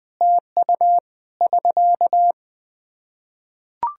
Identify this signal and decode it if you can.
Morse code, heard as a keyed tone, about 20 wpm, 700 Hz. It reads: TU <SK>
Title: Practice QSO - Exchange 4 - Ending - 20wpm